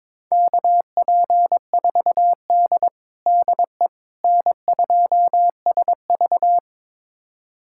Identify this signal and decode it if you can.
Morse code, heard as a keyed tone, about 21 wpm, 700 Hz. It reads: KP4D DE N2SV
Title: USA Callsign DE Callsign - 4 Characters - 22wpm